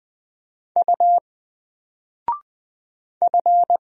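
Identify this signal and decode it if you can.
Morse code, heard as a keyed tone, about 20 wpm, 700 Hz. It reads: U F
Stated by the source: Single letter and number Morse drill at 20 wpm, from N4LCD.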